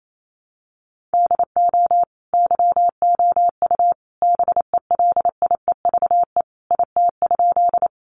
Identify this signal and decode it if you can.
Morse code, heard as a keyed tone, about 28 wpm, 700 Hz. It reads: DO YOU BELIEVE IT?
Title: Sentences from Top 1000 Words 28wpm